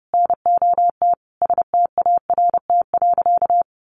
Morse code, heard as a keyed tone, about 30 wpm, 700 Hz. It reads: NOT START.
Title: Sentences from Top 1500 Words 30wpm